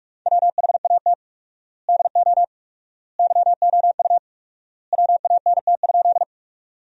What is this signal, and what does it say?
Morse code, heard as a keyed tone, about 41 wpm, 700 Hz. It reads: WHAT DO YOU WANT?
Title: 4x Word-Spacing - Sentences from Top 200 Words - 45wpm